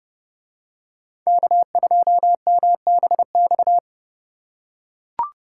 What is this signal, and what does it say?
Morse code, heard as a keyed tone, about 28 wpm, 700 Hz. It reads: K2MBX